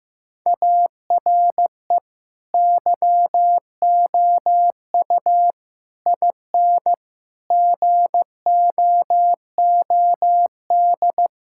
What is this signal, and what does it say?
Morse code, heard as a keyed tone, about 15 wpm, 700 Hz. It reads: ARE YOU IN GOOD
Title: Sentences from Top 1000 Words - Review Part 5 - 15wpm